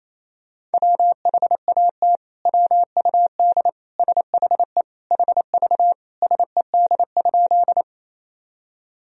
Morse code, heard as a keyed tone, about 27 wpm, 700 Hz. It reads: WHAT WUD SHE HV SED?
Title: Sentences from Top 100 Words and Abbreviations 28wpm